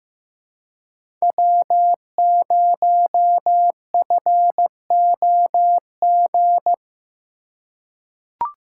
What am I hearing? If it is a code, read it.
Morse code, heard as a keyed tone, about 15 wpm, 700 Hz. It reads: W0FOG
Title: USA Call Signs - 5 Characters - 3x Character Spacing - 15wpm